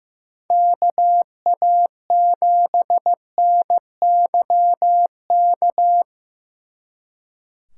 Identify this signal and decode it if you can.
Morse code, heard as a keyed tone, about 17 wpm, 700 Hz. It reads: KA7NYK
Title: Speed Racing - USA 2x3 Call Signs 15wpm